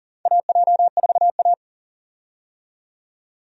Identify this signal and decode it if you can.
Morse code, heard as a keyed tone, about 40 wpm, 700 Hz. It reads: AJ4A
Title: USA Call Signs - 4 Characters - 40wpm